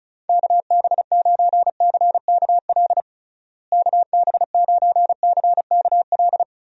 Morse code, heard as a keyed tone, about 36 wpm, 700 Hz. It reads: KB9CKL KB9CKL
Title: USA Call Signs - 6 Characters - Repeated 2x - 35wpm